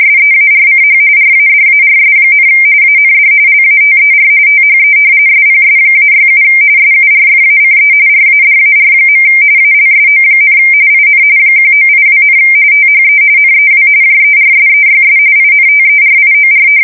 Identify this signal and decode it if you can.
RTTY, about 45.45 baud. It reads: SEARCH RADARS, AND LATER THE SPS-67 SURFACE RADAR. ALSO, IN 1982, THE NEW JERSEYS INTERIOR
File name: fsk170 / battleshipnj